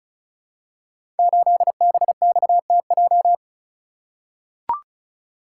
Morse code, heard as a keyed tone, about 36 wpm, 700 Hz. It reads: 8BXTJ
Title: Code Groups - 5 Letter:Numbers - 35wpm